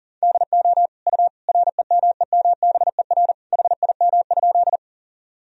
Morse code, heard as a keyed tone, about 40 wpm, 700 Hz. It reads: DO U REMEMBER HIM?
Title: Sentences from Top 400 Words and Abbreviations 40wpm